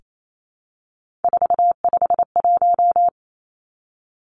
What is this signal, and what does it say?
Morse code, heard as a keyed tone, about 28 wpm, 700 Hz. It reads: 451
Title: Three Challenging Letter-Number Combinations 28wpm